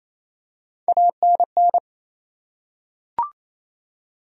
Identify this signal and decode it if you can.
Morse code, heard as a keyed tone, about 28 wpm, 700 Hz. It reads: ANN
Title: USA Common Names - Max 3 Letters - 28wpm